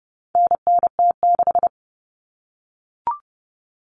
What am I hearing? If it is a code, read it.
Morse code, heard as a keyed tone, about 30 wpm, 700 Hz. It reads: NNT6
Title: Common Cut Numbers - 4-Digit 30wpm